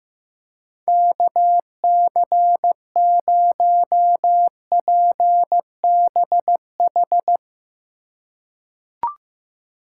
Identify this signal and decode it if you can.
Morse code, heard as a keyed tone, about 15 wpm, 700 Hz. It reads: KC0PBH
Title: USA Call Signs - 6 Characters - 2x Character Spacing - 15wpm